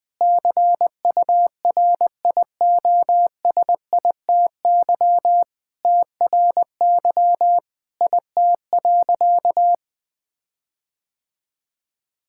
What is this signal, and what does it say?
Morse code, heard as a keyed tone, about 20 wpm, 700 Hz. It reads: CURIOSITY TRY IT.
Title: Sentences with 6-Word Phrases - 20wpm